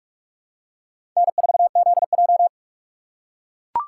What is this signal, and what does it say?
Morse code, heard as a keyed tone, about 44 wpm, 700 Hz. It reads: N4ZJ
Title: USA Call Signs - 4 Characters - 2x Character Spacing - 45wpm